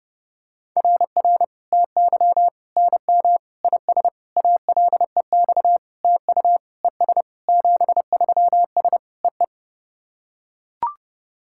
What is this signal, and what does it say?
Morse code, heard as a keyed tone, about 28 wpm, 700 Hz. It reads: RR TY NM IS ALEX TU ES 73S EE
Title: SOTA QSO Practice 30wpm